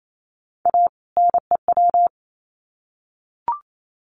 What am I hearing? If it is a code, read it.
Morse code, heard as a keyed tone, about 28 wpm, 700 Hz. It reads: A NEW
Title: Sets of 2 Words 28wpm